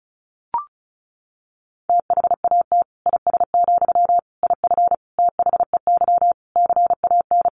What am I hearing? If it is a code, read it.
Morse code, heard as a keyed tone, about 34 wpm, 700 Hz. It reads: THAT IS, IF THEY CAN
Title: Sentences from Top 1100 Words 35wpm